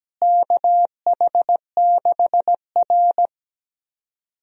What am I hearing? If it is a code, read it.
Morse code, heard as a keyed tone, about 17 wpm, 700 Hz. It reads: KH6R
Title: USA Call Signs - 4 Characters - Repeated 3x - 17wpm